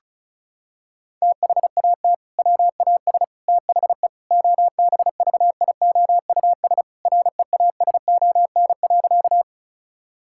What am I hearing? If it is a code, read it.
Morse code, heard as a keyed tone, about 34 wpm, 700 Hz. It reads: THAT WAS THE OBVIOUS REASON.
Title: Sentences from Top 1600 Words 35wpm